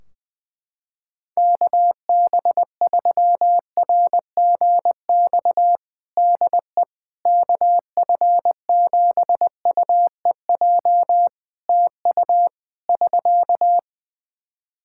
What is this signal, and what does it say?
Morse code, heard as a keyed tone, about 21 wpm, 700 Hz. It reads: KB3RGX DE KF7UEJ TU <SK>
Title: Practice QSO - Exchange 4 - Ending - 20wpm